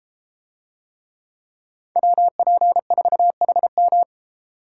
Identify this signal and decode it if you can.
Morse code, heard as a keyed tone, about 32 wpm, 700 Hz. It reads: WP4HM